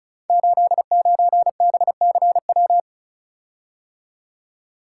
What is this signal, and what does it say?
Morse code, heard as a keyed tone, about 36 wpm, 700 Hz. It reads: 89BCW
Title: Code Groups - 5 Letter:Numbers - 35wpm